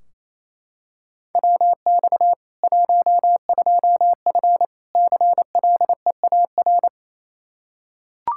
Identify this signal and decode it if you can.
Morse code, heard as a keyed tone, about 28 wpm, 700 Hz. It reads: WX 12F CLEAR
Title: Practice QSO - Exchange 2 - Info - 28wpm